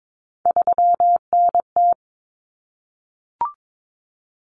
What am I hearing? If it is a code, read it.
Morse code, heard as a keyed tone, about 22 wpm, 700 Hz. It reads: 3NT
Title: Common Cut Numbers - 3-Digit 22wpm